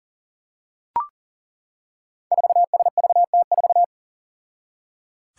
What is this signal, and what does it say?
Morse code, heard as a keyed tone, about 37 wpm, 700 Hz. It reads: 4SVT4
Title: Five Challenging Letter-Number Combinations 40wpm